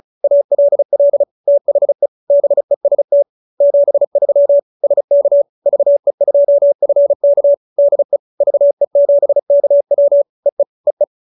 Morse code, heard as a keyed tone, about 34 wpm, 570 Hz. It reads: ALL THE BEST 73 SK VE2FK DE VE7KW EE EE
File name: QSO314 35